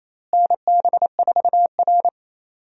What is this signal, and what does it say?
Morse code, heard as a keyed tone, about 28 wpm, 700 Hz. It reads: NB4R
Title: USA Call Signs - 4 Characters - Repeated 2x - 28wpm